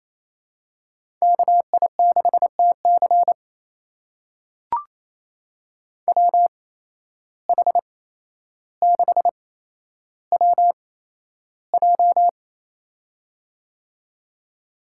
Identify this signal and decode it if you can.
Morse code, heard as a keyed tone, about 27 wpm, spaced as for 12 wpm, 700 Hz. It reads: KI6TC WH6WJ
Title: USA Call Signs - 5 Characters - 8x Character Spacing - 28wpm